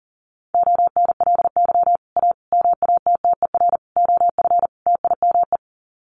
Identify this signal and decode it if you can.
Morse code, heard as a keyed tone, about 40 wpm, 700 Hz. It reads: ONLY A MATTER OF TIME
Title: Sets of 5 Words Encore 40wpm